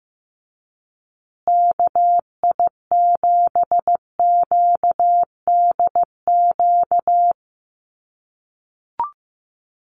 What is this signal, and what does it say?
Morse code, heard as a keyed tone, about 15 wpm, 700 Hz. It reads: KI7QDQ
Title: USA Call Signs Repeated 2x 15wpm